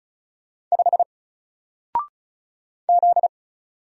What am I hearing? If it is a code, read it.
Morse code, heard as a keyed tone, about 36 wpm, 700 Hz. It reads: 5 Z
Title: Single Letter-Number 35wpm